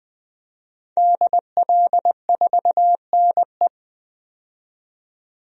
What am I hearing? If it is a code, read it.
Morse code, heard as a keyed tone, about 20 wpm, 700 Hz. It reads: DL4NE
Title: DXCC-100 Call Signs - 5 Characters - 20wpm